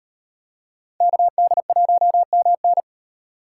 Morse code, heard as a keyed tone, about 38 wpm, 700 Hz. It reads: KD1MN